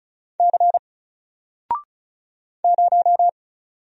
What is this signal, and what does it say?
Morse code, heard as a keyed tone, about 36 wpm, 700 Hz. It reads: C 0
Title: Single Character (Letter:Number:Punctuation:Prosign) Alt 35wpm